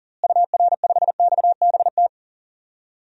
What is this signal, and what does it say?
Morse code, heard as a keyed tone, about 37 wpm, 700 Hz. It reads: UR5XBT
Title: DXCC-100 Call Signs - 6 Characters - Repeated 3x - 40wpm